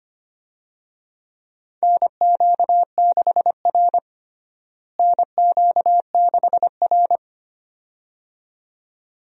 Morse code, heard as a keyed tone, about 25 wpm, 700 Hz. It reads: NQ6R NQ6R
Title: USA Call Signs - 4 Characters - Repeated 2x - 25wpm